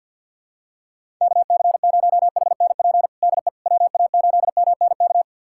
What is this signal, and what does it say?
Morse code, heard as a keyed tone, about 47 wpm, 700 Hz. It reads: KC0HNP DE WA8MNK
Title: USA Callsign DE Callsign 50wpm